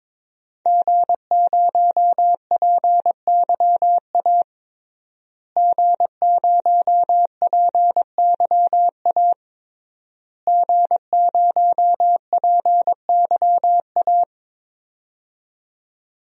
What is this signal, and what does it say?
Morse code, heard as a keyed tone, about 22 wpm, 700 Hz. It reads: G0PYA G0PYA G0PYA
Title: DXCC-100 Call Signs - 5 Characters - Repeated 3x - 22wpm